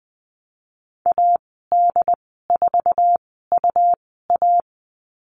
Morse code, heard as a keyed tone, about 20 wpm, spaced as for 15 wpm, 700 Hz. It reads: AD4UA
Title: 179 Lesson 29g - Slash - Call Signs 20wpm